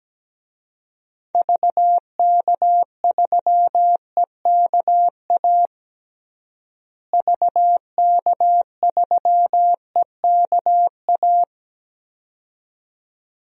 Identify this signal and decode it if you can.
Morse code, heard as a keyed tone, about 17 wpm, 700 Hz. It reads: VK3EKA VK3EKA
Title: DXCC-100 Call Signs - 6 Characters - Repeated 2x - 17wpm